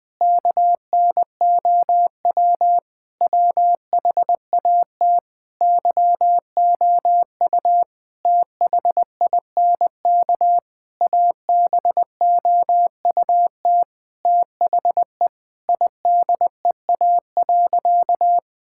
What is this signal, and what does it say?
Morse code, heard as a keyed tone, about 20 wpm, 700 Hz. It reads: KNOW WHAT YOU THINK ABOUT THE IDEA.
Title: Sentences with 6-Word Phrases - 20wpm